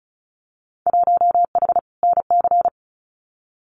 Morse code, heard as a keyed tone, about 34 wpm, 700 Hz. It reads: 1H NC
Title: Winter Field Day Exchanges 35wpm